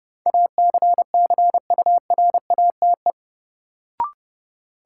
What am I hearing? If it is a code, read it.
Morse code, heard as a keyed tone, about 30 wpm, 700 Hz. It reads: ACCURATE
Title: Sentences from Top 1200 Words - New Word Drill Alt - 30wpm